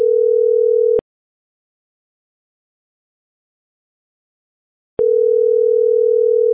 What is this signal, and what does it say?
Ringback tone (North America), 440 + 480 Hz.